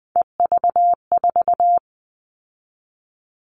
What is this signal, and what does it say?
Morse code, heard as a keyed tone, about 20 wpm, 700 Hz. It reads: EV4
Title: All Cut Numbers - 3-Digit 20wpm